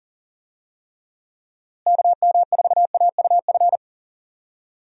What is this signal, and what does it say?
Morse code, heard as a keyed tone, about 40 wpm, 700 Hz. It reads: KM4AUF
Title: USA Call Signs - 6 Characters - Repeated 3x - 40wpm